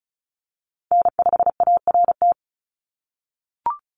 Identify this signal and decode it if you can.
Morse code, heard as a keyed tone, about 36 wpm, 700 Hz. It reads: N5ART
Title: USA Call Signs 35wpm